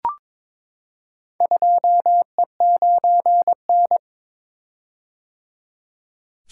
Morse code, heard as a keyed tone, about 22 wpm, 700 Hz. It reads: 2E9N
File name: practice / 22wpm / Four Letter-Number Combinations - 22wpm